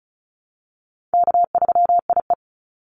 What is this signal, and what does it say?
Morse code, heard as a keyed tone, about 34 wpm, 700 Hz. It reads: K3IE
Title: Call Signs Repeated 3x - 35wpm